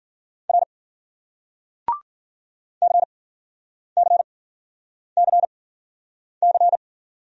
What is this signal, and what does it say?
Morse code, heard as a keyed tone, about 49 wpm, 700 Hz. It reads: R C C C C
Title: Speed-Racing Single Letters 40wpm